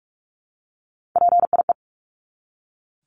Morse code, heard as a keyed tone, about 44 wpm, 700 Hz. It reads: PIE
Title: Top 1751-2000 Words 45wpm